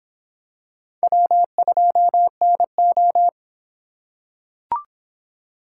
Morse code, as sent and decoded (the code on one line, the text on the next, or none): .-- ..--- -. ---
W2NO